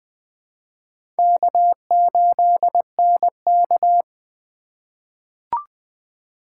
Morse code, heard as a keyed tone, about 20 wpm, 700 Hz. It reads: K8NK
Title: USA Call Signs - 4 Characters - 3x Character Spacing - 20wpm